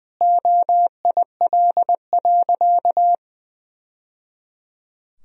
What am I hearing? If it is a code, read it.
Morse code, heard as a keyed tone, about 20 wpm, 700 Hz. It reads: OIL.
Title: Sentences from Top 1000 Words - Review Part 4 - 20wpm